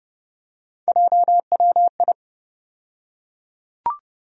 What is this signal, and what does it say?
Morse code, heard as a keyed tone, about 32 wpm, 700 Hz. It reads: JWI
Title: USA Call Sign Postfixes - 3 Letters 30wpm